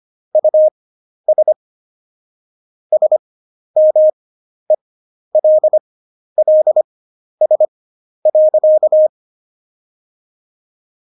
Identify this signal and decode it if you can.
Morse code, heard as a keyed tone, about 25 wpm, spaced as for 11 wpm, 620 Hz. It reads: US SMELLS.
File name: Jokes-25-char-10-eff